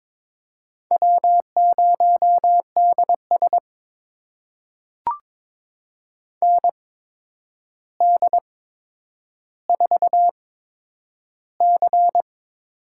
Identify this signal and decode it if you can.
Morse code, heard as a keyed tone, about 22 wpm, spaced as for 9 wpm, 700 Hz. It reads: W0DS ND4C
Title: USA Call Signs - 4 Characters - 8x Character Spacing - 22wpm